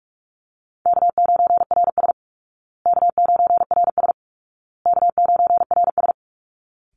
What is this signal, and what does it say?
Morse code, heard as a keyed tone, about 44 wpm, 700 Hz. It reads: K9RS K9RS K9RS
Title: Easy Call Signs Repeated 3x - 45wpm